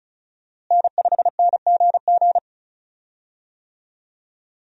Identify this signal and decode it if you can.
Morse code, heard as a keyed tone, about 36 wpm, 700 Hz. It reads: N5NGG
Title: USA Call Signs - 5 Characters - Repeated 3x - 35wpm